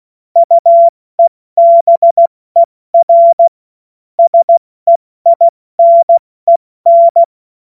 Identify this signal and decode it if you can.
Morse code, heard as a keyed tone, about 16 wpm, 680 Hz. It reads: UEBER SEINEN
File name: Morsekurs / Witze / Witz 19 Dezember 2023 70 F80